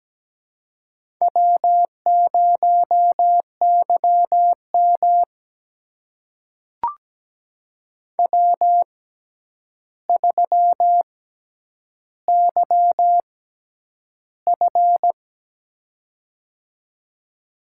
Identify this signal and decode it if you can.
Morse code, heard as a keyed tone, about 17 wpm, spaced as for 9 wpm, 700 Hz. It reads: W0YM W3YF
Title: USA Call Signs - 4 Characters - 6x Character Spacing - 17wpm